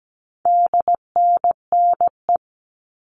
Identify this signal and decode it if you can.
Morse code, heard as a keyed tone, about 17 wpm, 700 Hz. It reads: DNNE
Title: All Cut Numbers - 4-Digit 17wpm